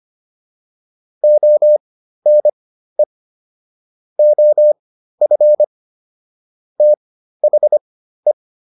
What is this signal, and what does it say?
Morse code, heard as a keyed tone, about 25 wpm, spaced as for 13 wpm, 600 Hz. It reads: ONE OF THE